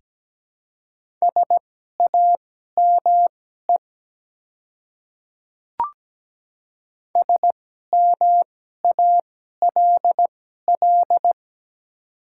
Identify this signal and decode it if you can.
Morse code, heard as a keyed tone, about 17 wpm, spaced as for 12 wpm, 700 Hz. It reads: SAME SMALL
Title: 2x Character Spacing - Top 200 Words - 17wpm